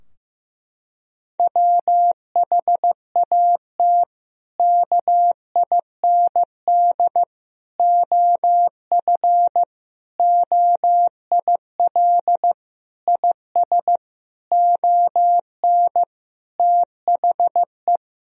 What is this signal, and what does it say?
Morse code, heard as a keyed tone, about 15 wpm, 700 Hz. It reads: WHAT KIND OF OIL IS ON THE
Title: Sentences from Top 1000 Words - Review Part 4 - 15wpm